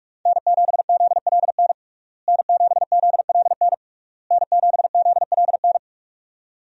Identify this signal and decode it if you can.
Morse code, heard as a keyed tone, about 41 wpm, 700 Hz. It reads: N7ZLN N7ZLN N7ZLN
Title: USA Call Signs - 5 Characters - Repeated 3x - 45wpm